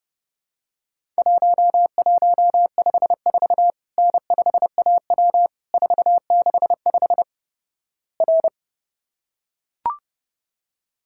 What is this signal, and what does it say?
Morse code, heard as a keyed tone, about 30 wpm, 700 Hz. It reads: 1154 N5AW 465 R
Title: Worked All Europe DX Contest - QTC Exchange - 30wpm